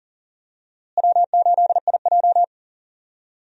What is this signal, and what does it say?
Morse code, heard as a keyed tone, about 40 wpm, 700 Hz. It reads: W8IJ
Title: USA Call Signs - 4 Characters - Repeated 2x - 40wpm